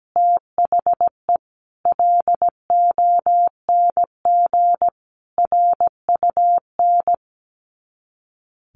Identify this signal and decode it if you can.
Morse code, heard as a keyed tone, about 17 wpm, 700 Hz. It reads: THE LONG RUN
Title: Sets of 4 Words 17wpm